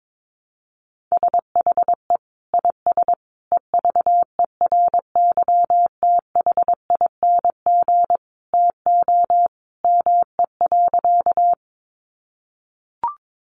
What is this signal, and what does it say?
Morse code, heard as a keyed tone, about 22 wpm, 700 Hz. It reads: SHE IS EVERYTHING TO ME.
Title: Sentences from Top 700 Words 22wpm